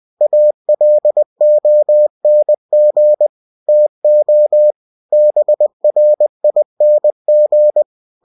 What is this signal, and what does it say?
Morse code, heard as a keyed tone, about 20 wpm, 600 Hz. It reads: ALONG TO BRING
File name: cw-books / winnie-the-pooh / winnie-20wpm-page-0055